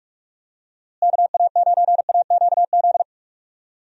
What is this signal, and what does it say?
Morse code, heard as a keyed tone, about 45 wpm, 700 Hz. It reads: KA9AQZ
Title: USA Call Signs - 6 Characters - Repeated 3x - 45wpm